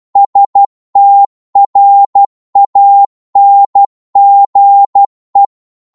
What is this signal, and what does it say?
Morse code, heard as a keyed tone, about 12 wpm, 800 Hz. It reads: STRANGE